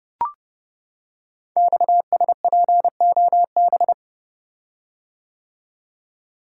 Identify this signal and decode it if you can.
Morse code, heard as a keyed tone, about 29 wpm, 700 Hz. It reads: XSPOB
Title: Code Groups - 5 Letters - 30wpm